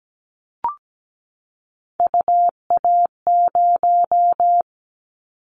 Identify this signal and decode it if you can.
Morse code, heard as a keyed tone, about 17 wpm, 700 Hz. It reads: UA0
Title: Callsign Prefixes for Top 25 DXCC - 17wpm